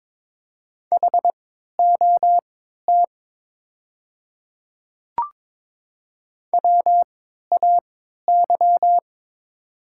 Morse code, heard as a keyed tone, about 22 wpm, spaced as for 12 wpm, 700 Hz. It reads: HOT WAY
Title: 3x Character Spacing - Top 100 Words - 22wpm